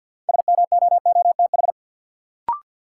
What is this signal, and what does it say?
Morse code, heard as a keyed tone, about 53 wpm, 700 Hz. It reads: SMOOTH
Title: Sentences from Top 1400 Words - New Word Drill Alt - 50wpm